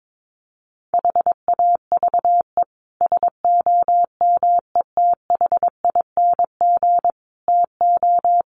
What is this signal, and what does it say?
Morse code, heard as a keyed tone, about 22 wpm, 700 Hz. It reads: HAVE SOMETHING TO
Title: Sets of 5 Words Encore 22wpm